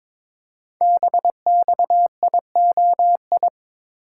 Morse code, heard as a keyed tone, about 22 wpm, 700 Hz. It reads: BXIOI
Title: Code Groups - 5 Letters - 22wpm